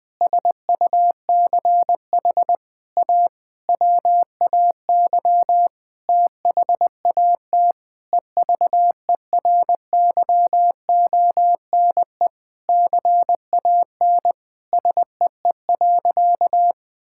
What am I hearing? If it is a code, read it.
Morse code, heard as a keyed tone, about 20 wpm, 700 Hz. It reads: SUCH A WAY THAT EVERYONE CAN SEE.
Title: Sentences with 5-Word Phrases - Spoken Beginning - 20wpm